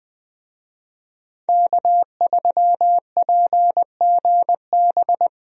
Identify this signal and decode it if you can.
Morse code, heard as a keyed tone, about 20 wpm, 700 Hz. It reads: K3PGB
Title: USA Call Signs - 5 Characters - 20wpm